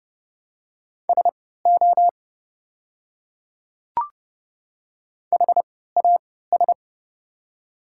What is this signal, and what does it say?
Morse code, heard as a keyed tone, about 30 wpm, spaced as for 17 wpm, 700 Hz. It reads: SO HAS
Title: 3x Character Spacing - Top 100 Words - 30wpm